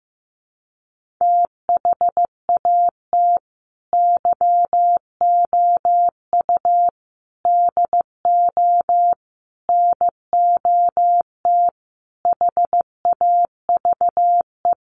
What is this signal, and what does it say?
Morse code, heard as a keyed tone, about 15 wpm, 700 Hz. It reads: THAT YOU DO NOT HAVE
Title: Sets of 5 Words Encore 15wpm